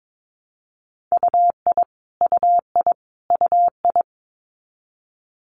Morse code, heard as a keyed tone, about 22 wpm, 700 Hz. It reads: UI UI UI
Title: Multi-letter Phonemes Repeated 3x 22wpm